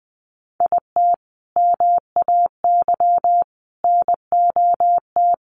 Morse code, heard as a keyed tone, about 20 wpm, 700 Hz. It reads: IT MAY NOT
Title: Sentences from Top 1400 Words 20wpm